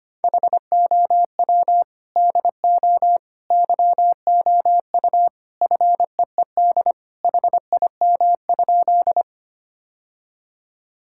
Morse code, heard as a keyed tone, about 25 wpm, 700 Hz. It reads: HOW DO YOU FEED HIM?
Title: Sentences from Top 1600 Words 25wpm